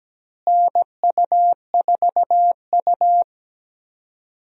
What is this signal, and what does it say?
Morse code, heard as a keyed tone, about 17 wpm, 700 Hz. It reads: NU4U
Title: USA Call Signs - 4 Characters - Repeated 3x - 17wpm